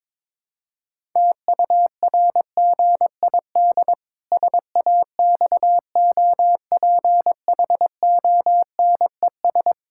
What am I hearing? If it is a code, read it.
Morse code, heard as a keyed tone, about 22 wpm, 700 Hz. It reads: TURGID SAXOPHONES
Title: Pangram Challenge v2 - 22wpm